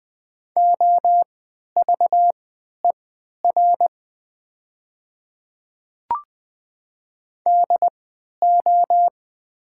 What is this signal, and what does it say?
Morse code, heard as a keyed tone, about 20 wpm, spaced as for 11 wpm, 700 Hz. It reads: OVER DO
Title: 3x Character Spacing - Top 100 Words - 20wpm